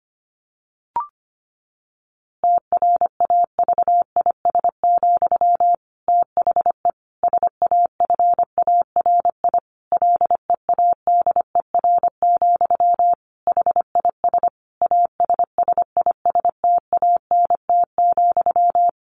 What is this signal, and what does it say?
Morse code, heard as a keyed tone, about 25 wpm, 700 Hz. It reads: TRAVIS, THE SAFARI LEADER, HIS ASSISTANT,
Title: A Sound Of Thunder 25wpm